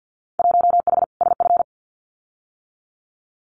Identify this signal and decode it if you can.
Morse code, heard as a keyed tone, about 50 wpm, 700 Hz. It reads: 1H SF
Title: Winter Field Day Exchanges 50wpm